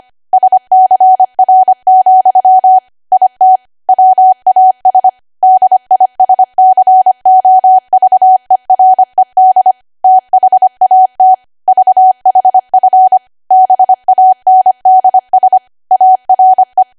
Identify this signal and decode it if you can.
Morse code, heard as a keyed tone, about 25 wpm, 750 Hz. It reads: SCR, IT WAS DISCOVERED THAT VHF BANDS ARE
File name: Archive / 25 WPM / 25WPM